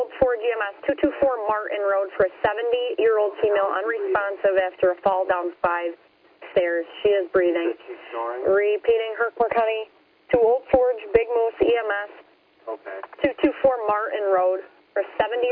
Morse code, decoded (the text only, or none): none